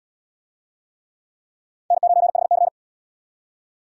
Morse code, heard as a keyed tone, about 78 wpm, 700 Hz. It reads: N9HC